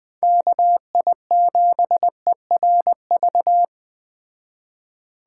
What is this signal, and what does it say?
Morse code, heard as a keyed tone, about 20 wpm, 700 Hz. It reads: KI7ERV